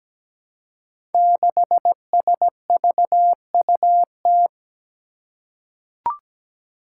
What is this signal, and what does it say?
Morse code, heard as a keyed tone, about 17 wpm, 700 Hz. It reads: 6SVUT
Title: Five Challenging Letter-Number Combinations 17wpm